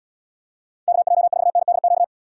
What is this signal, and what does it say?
Morse code, heard as a keyed tone, about 70 wpm, 700 Hz. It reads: KJ4NDQ